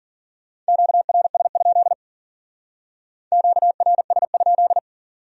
Morse code, heard as a keyed tone, about 48 wpm, 700 Hz. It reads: QRS? QRS?
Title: Speed-Racing Q-Codes - 40wpm